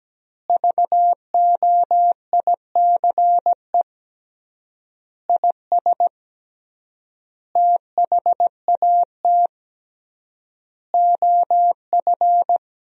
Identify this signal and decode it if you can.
Morse code, heard as a keyed tone, about 17 wpm, 700 Hz. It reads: VOICE IS THAT OF
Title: 3x Word-Spacing - Sentences from Top 500 Words - 17wpm